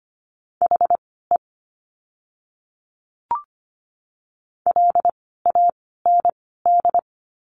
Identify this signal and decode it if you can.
Morse code, heard as a keyed tone, about 25 wpm, 700 Hz. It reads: H E L A N D
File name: course / 25:10wpm / 094 Lesson 16f - Letter c - Words 25wpm